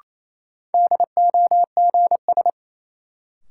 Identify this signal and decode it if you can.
Morse code, heard as a keyed tone, about 28 wpm, 700 Hz. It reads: DOGS
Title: Sentences from Top 1600 Words - New Word Drill - 28wpm